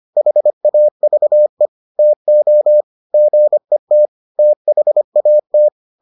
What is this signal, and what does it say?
Morse code, heard as a keyed tone, about 25 wpm, 600 Hz. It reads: HAVE TO GET THAT